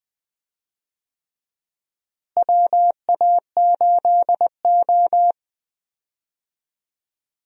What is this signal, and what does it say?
Morse code, heard as a keyed tone, about 20 wpm, 700 Hz. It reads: WA8O
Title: USA Call Signs - 4 Characters - Repeated 3x - 20wpm